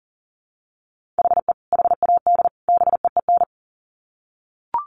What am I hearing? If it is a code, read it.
Morse code, heard as a keyed tone, about 40 wpm, 700 Hz. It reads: HE HAD BEEN
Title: Sets of 3 Words 40wpm